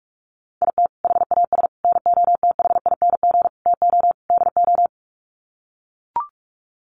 Morse code, heard as a keyed tone, about 44 wpm, 700 Hz. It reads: IT HAS NOTHING TO DO
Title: Sets of 5 Words Encore 45wpm